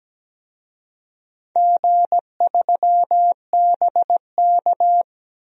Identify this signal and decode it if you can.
Morse code, heard as a keyed tone, about 17 wpm, 700 Hz. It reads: G3BK
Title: DXCC-100 Call Signs - 4 Characters - 17wpm